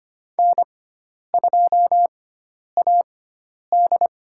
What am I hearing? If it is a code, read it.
Morse code, heard as a keyed tone, about 25 wpm, spaced as for 10 wpm, 700 Hz. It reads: N2AD
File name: Lesson 16g - Letter c - Call Signs 25wpm